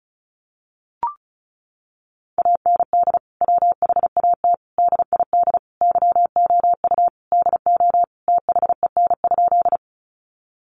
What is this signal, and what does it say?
Morse code, heard as a keyed tone, about 34 wpm, 700 Hz. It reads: AND WHAT DID YOU DO THEN?
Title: Sentences from Top 100 Words 35wpm